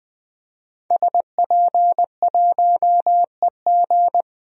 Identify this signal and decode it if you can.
Morse code, heard as a keyed tone, about 20 wpm, 700 Hz. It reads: SP1EG
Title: DXCC-100 Call Signs - 5 Characters - 20wpm